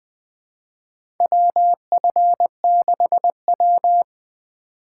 Morse code, heard as a keyed tone, about 20 wpm, 700 Hz. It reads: WF6W
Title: USA Call Signs - 4x Character Spacing - 20wpm